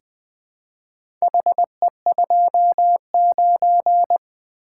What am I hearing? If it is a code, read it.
Morse code, heard as a keyed tone, about 20 wpm, 700 Hz. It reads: HE29
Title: Four Letter-Number Combinations - 20wpm